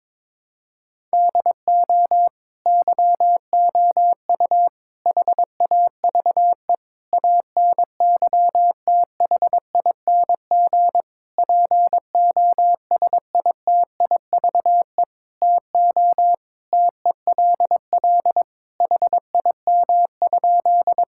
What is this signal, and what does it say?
Morse code, heard as a keyed tone, about 22 wpm, 700 Hz. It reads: DO YOU HAVE ANYTHING POSITIVE TO TELL HIM?
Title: Sentences from Top 1000 Words - Review Part 5 - 22wpm